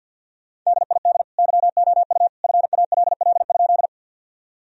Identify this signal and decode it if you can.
Morse code, heard as a keyed tone, about 50 wpm, 700 Hz. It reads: DID YOU FALL?
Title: Sentences from Top 1000 Words - Review Part 2 - 50wpm